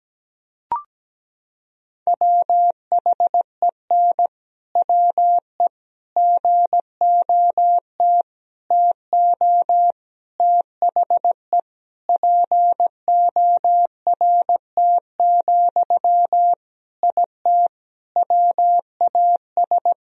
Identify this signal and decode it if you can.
Morse code, heard as a keyed tone, about 17 wpm, 700 Hz. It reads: WHEN WE GOT TO THE PORT, IT WAS
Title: Sentences from Top 1000 Words - Review Part 1 - 17wpm